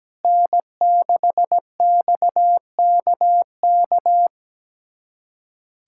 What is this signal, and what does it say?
Morse code, heard as a keyed tone, about 17 wpm, 700 Hz. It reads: N6XKK
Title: USA Callsign DE Callsign - 5 Characters - 17wpm